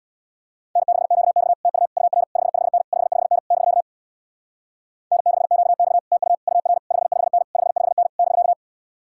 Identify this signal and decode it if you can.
Morse code, heard as a keyed tone, about 62 wpm, 690 Hz. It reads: N6QX TU UR 55N 55N <BK> N6QX TU UR 55N 55N <BK>
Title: Speed-Racing Common QSO Phrases - 50wpm